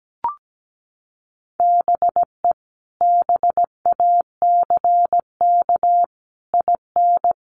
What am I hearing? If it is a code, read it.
Morse code, heard as a keyed tone, about 17 wpm, 700 Hz. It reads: BE BACK IN
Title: Sets of 5 Words 17wpm